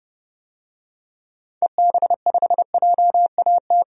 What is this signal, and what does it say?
Morse code, heard as a keyed tone, about 30 wpm, 700 Hz. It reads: EB5JAT
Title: DXCC-100 Call Signs - 6 Characters - 30wpm